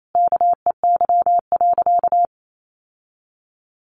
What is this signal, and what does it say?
Morse code, heard as a keyed tone, about 28 wpm, 700 Hz. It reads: KEY.
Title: Sentences from Top 1400 Words 28wpm